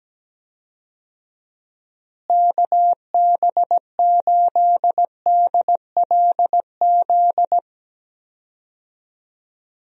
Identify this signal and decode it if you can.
Morse code, heard as a keyed tone, about 17 wpm, 700 Hz. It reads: KB8DLZ
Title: USA Call Signs - 6 Characters - Repeated 3x - 17wpm